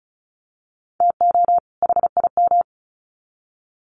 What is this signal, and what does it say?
Morse code, heard as a keyed tone, about 34 wpm, 700 Hz. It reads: TO HIM
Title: Sets of 2 Words 35wpm